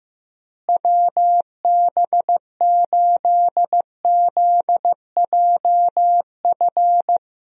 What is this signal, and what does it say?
Morse code, heard as a keyed tone, about 16 wpm, 700 Hz. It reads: WB8ZJF
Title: USA Call Signs - 6 Characters - Repeated 3x - 15wpm